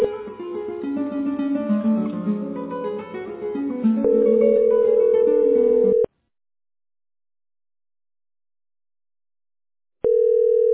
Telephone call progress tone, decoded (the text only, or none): ringback tone